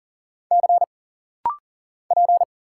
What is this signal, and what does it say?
Morse code, heard as a keyed tone, about 40 wpm, 700 Hz. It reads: C P
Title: Single Letter-Number - Rapid-Fire 40wpm